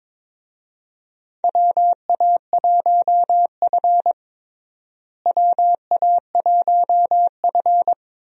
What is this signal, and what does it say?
Morse code, heard as a keyed tone, about 21 wpm, 700 Hz. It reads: WA1F WA1F
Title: USA Call Signs - 4 Characters - Repeated 2x - 22wpm